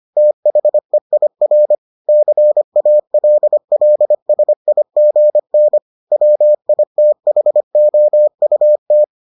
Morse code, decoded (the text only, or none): THEIR CALLSIGN WITHOUT